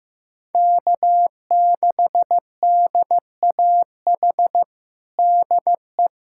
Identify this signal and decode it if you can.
Morse code, heard as a keyed tone, about 16 wpm, 700 Hz. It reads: K6DAH DE
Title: USA Callsign DE Callsign - 5 Characters - 15wpm